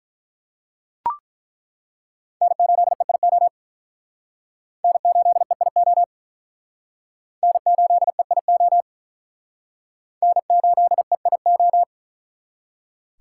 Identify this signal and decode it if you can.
Morse code, heard as a keyed tone, about 48 wpm, 700 Hz. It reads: N8EIO N8EIO N8EIO N8EIO